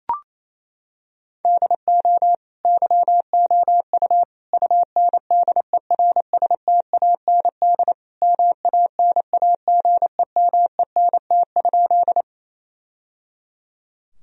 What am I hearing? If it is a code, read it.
Morse code, heard as a keyed tone, about 28 wpm, 700 Hz. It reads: DO YOU UNDERSTAND MANAGEMENT?
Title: Sentences from Top 1000 Words - Review Part 4 - 28wpm